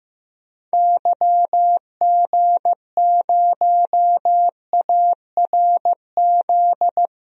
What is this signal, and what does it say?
Morse code, heard as a keyed tone, about 15 wpm, 700 Hz. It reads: YG0ARZ
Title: DXCC-100 Call Signs - 6 Characters - 15wpm